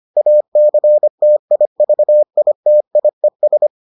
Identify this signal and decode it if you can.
Morse code, heard as a keyed tone, about 25 wpm, 600 Hz. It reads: ACTIVITIES